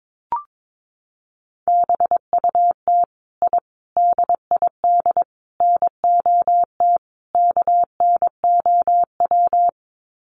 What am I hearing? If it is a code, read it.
Morse code, heard as a keyed tone, about 22 wpm, 700 Hz. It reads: BUT I DID NOT KNOW
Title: Sets of 5 Words Encore 22wpm